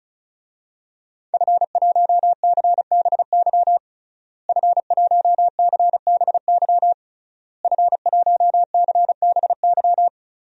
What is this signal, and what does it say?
Morse code, heard as a keyed tone, about 34 wpm, 700 Hz. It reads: F1CBY F1CBY F1CBY
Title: DXCC-100 Call Signs - 5 Characters - Repeated 3x - 35wpm